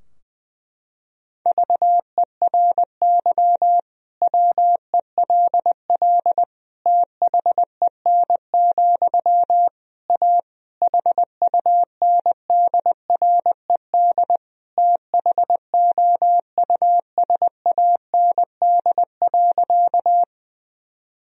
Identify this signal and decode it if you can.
Morse code, heard as a keyed tone, about 21 wpm, 700 Hz. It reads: VERY WELL THEN, A HUNDRED THOUSAND.
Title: Sentences from Top 1000 Words - Review Part 3 - 20wpm